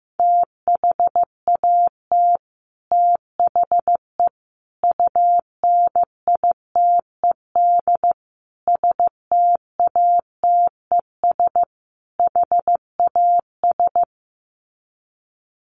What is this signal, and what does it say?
Morse code, heard as a keyed tone, about 15 wpm, 700 Hz. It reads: THAT THE UNITED STATES HAS
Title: Sets of 5 Words Encore 15wpm